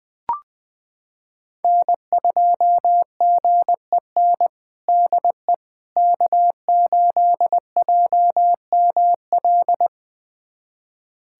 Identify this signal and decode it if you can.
Morse code, heard as a keyed tone, about 20 wpm, 700 Hz. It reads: N2GEN DE K8JML
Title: USA Callsign DE Callsign - 5 Characters - 20wpm